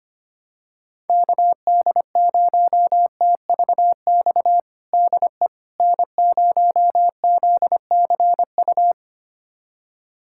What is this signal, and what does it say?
Morse code, heard as a keyed tone, about 25 wpm, 700 Hz. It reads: KD0TVX DE N0ZCU
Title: USA Callsign DE Callsign 25wpm